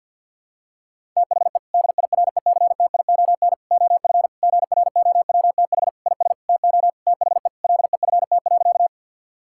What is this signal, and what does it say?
Morse code, heard as a keyed tone, about 52 wpm, 690 Hz. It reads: THE DIRECTION OF GROWTH IS TO THE LEFT.
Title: Sentences from Top 1000 Words - Review Part 5 - 50wpm